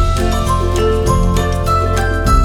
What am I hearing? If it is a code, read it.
There is no code.